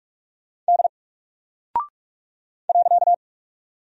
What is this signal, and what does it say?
Morse code, heard as a keyed tone, about 46 wpm, 700 Hz. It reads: D .
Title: Single Character (Letter:Number:Punctuation:Prosign) Alt 45wpm